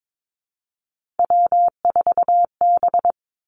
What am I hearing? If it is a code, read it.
Morse code, heard as a keyed tone, about 22 wpm, 700 Hz. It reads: W4B
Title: Three Challenging Letter-Number Combinations 22wpm